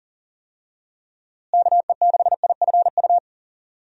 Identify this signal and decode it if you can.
Morse code, heard as a keyed tone, about 40 wpm, 700 Hz. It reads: KE6IFU
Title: USA Call Signs - 6 Characters - Repeated 2x - 40wpm